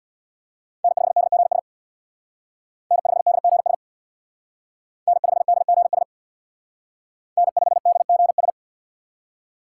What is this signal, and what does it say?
Morse code, heard as a keyed tone, about 62 wpm, 700 Hz. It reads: N5DGS N5DGS N5DGS N5DGS
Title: Speed Racing - USA 1x3 Call Signs 50wpm